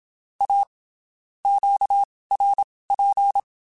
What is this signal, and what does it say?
Morse code, heard as a keyed tone, about 27 wpm, spaced as for 19 wpm, 800 Hz. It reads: A QRP
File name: (spaced) / QSOs Spaced-mp3s / WD0BGP 1 @25 sp